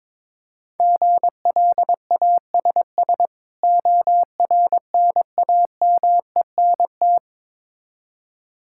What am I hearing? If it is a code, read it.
Morse code, heard as a keyed tone, about 22 wpm, 690 Hz. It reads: GLASS ORNAMENT
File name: Difficult Words with Hints - 22wpm